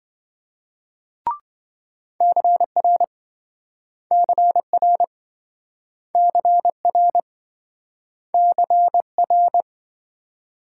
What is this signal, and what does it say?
Morse code, heard as a keyed tone, about 28 wpm, 700 Hz. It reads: CR CR CR CR